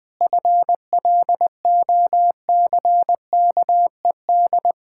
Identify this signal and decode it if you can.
Morse code, heard as a keyed tone, about 20 wpm, 700 Hz. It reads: FLOCKED
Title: Pangram Challenge v2 - 20wpm